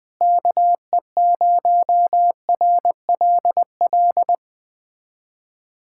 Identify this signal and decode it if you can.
Morse code, heard as a keyed tone, about 20 wpm, 700 Hz. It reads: KE0RLL
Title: USA Callsign DE Callsign - 6 Characters - 20wpm